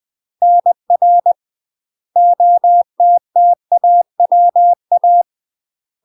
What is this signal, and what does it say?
Morse code, heard as a keyed tone, about 20 wpm, 700 Hz. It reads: NR OTTAWA
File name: UK MASTER QSOs TRAINING/20WPM